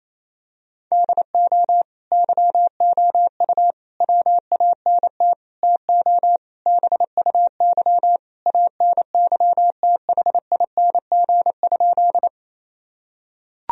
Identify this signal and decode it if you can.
Morse code, heard as a keyed tone, about 28 wpm, 700 Hz. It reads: DO YOU WANT TO BUY ANYTHING?
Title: Sentences from Top 1000 Words - Review Part 4 - 28wpm